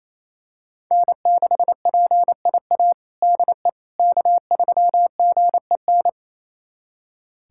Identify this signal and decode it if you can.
Morse code, heard as a keyed tone, about 28 wpm, 700 Hz. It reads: N6PIA DE K3GEN
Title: USA Callsign DE Callsign - 5 Characters - 28wpm